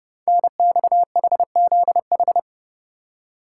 Morse code, heard as a keyed tone, about 29 wpm, 700 Hz. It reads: NXHZH
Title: Code Groups - 5 Letters - 30wpm